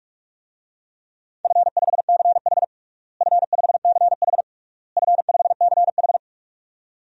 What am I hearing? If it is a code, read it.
Morse code, heard as a keyed tone, about 46 wpm, 700 Hz. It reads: F5CH F5CH F5CH
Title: DXCC-100 Call Signs - 4 Characters - Repeated 3x - 45wpm